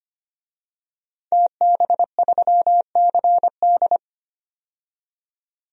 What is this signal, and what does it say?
Morse code, heard as a keyed tone, about 25 wpm, 700 Hz. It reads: TB3CD
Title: DXCC-100 Call Signs - 5 Characters - 25wpm